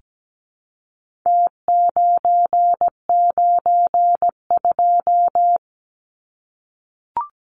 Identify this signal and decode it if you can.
Morse code, heard as a keyed tone, about 17 wpm, 700 Hz. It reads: T992
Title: Numbers - Contest Style - 4-Digit 17wpm